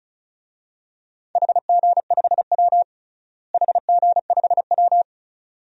Morse code, heard as a keyed tone, about 36 wpm, 700 Hz. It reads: HG5W HG5W
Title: DXCC-100 Call Signs - 4 Characters - Repeated 2x - 35wpm